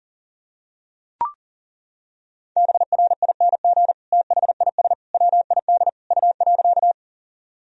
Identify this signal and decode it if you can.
Morse code, heard as a keyed tone, about 40 wpm, 700 Hz. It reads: BRING THIS WID U.
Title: Sentences from Top 500 Words and Abbreviations 40wpm